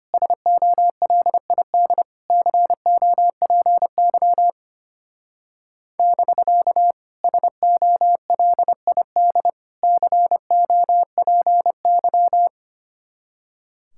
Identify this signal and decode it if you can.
Morse code, heard as a keyed tone, about 30 wpm, 700 Hz. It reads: SOLID COPY <BK> SOLID COPY
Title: Speed-Racing Common QSO Phrases Encore - 25wpm